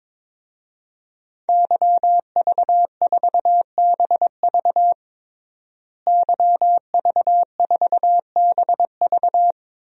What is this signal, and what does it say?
Morse code, heard as a keyed tone, about 22 wpm, 700 Hz. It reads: YV4BV YV4BV
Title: DXCC-100 Call Signs - 5 Characters - Repeated 3x - 22wpm